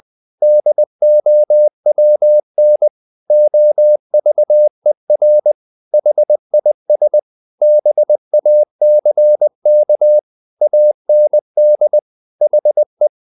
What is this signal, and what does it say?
Morse code, heard as a keyed tone, about 20 wpm, 600 Hz. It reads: DOWN OVER HIS BACK AND HE
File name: the old man and the sea / the old man and the sea part 1 20wpm